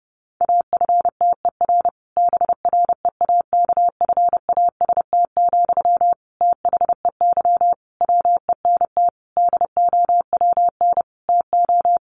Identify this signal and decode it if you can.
Morse code, heard as a keyed tone, about 30 wpm, 700 Hz. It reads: AFTER BREAKFAST, THEY WENT DOWN TO
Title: Sentences from Top 1500 Words 30wpm